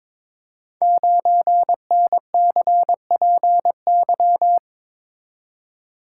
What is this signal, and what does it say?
Morse code, heard as a keyed tone, about 22 wpm, 700 Hz. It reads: 9NCPY
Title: Code Groups - 5 Letter:Numbers - 22wpm